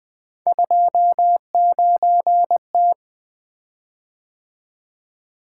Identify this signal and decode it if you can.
Morse code, heard as a keyed tone, about 20 wpm, 700 Hz. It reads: 29T
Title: Three Letter-Number Combinations - 20wpm